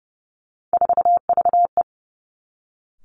Morse code, heard as a keyed tone, about 30 wpm, 700 Hz. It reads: 4VE
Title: Three Challenging Letter-Number Combinations 30wpm